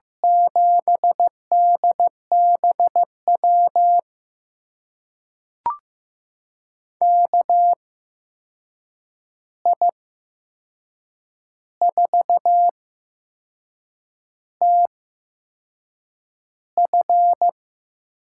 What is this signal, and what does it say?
Morse code, heard as a keyed tone, about 15 wpm, spaced as for 6 wpm, 700 Hz. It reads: 7DBW KI4TF
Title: USA Call Signs - 6 Characters - 8x Character Spacing - 15wpm